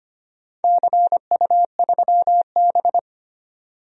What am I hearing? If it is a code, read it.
Morse code, heard as a keyed tone, about 25 wpm, 700 Hz. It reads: CU3B
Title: DXCC-100 Call Signs - 4 Characters - 25wpm